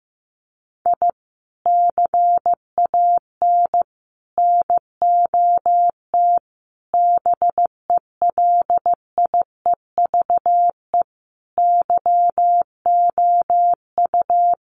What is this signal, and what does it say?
Morse code, heard as a keyed tone, about 15 wpm, 700 Hz. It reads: I CAN NOT BELIEVE YOU
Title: Sets of 5 Words Encore 15wpm